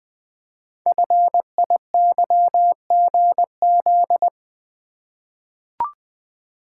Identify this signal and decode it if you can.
Morse code, heard as a keyed tone, about 20 wpm, 700 Hz. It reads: FIYGZ
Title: Code Groups - 5 Letters - 20wpm